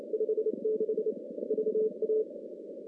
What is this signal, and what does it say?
Morse code, heard as a keyed tone, about 27 wpm, 410 Hz. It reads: 56 VA
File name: contests / CW-SS-k5tr- -2100